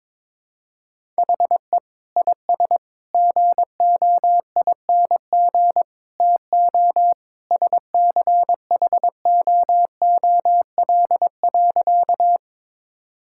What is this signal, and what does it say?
Morse code, heard as a keyed tone, about 22 wpm, 700 Hz. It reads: HE IS GOING TO SCHOOL.
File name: practice / 22wpm / Sentences with 3-Word Phrases - 22wpm